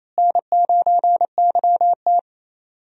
Morse code, heard as a keyed tone, about 28 wpm, 700 Hz. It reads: N9YT
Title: USA Call Signs - 4 Characters - Repeated 2x - 28wpm